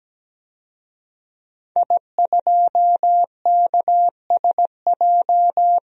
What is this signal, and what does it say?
Morse code, heard as a keyed tone, about 17 wpm, 700 Hz. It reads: I2KSJ
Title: DXCC-100 Call Signs - 5 Characters - 17wpm